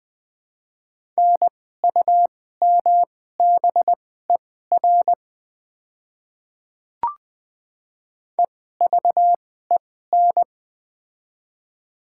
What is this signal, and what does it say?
Morse code, heard as a keyed tone, about 21 wpm, spaced as for 15 wpm, 700 Hz. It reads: NUMBER EVEN
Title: 2x Character Spacing - Top 200 Words - 20wpm